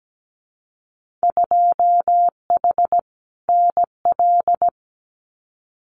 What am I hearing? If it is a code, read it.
Morse code, heard as a keyed tone, about 17 wpm, 700 Hz. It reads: 2H NL
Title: Winter Field Day Exchanges Encore 17wpm